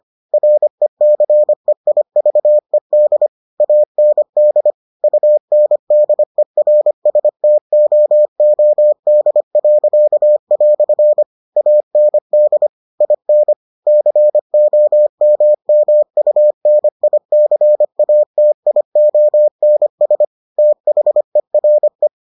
Morse code, heard as a keyed tone, about 25 wpm, 600 Hz. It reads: RECEIVED AND UNDERSTOOD." AND IN COMMUNICATIONS THERE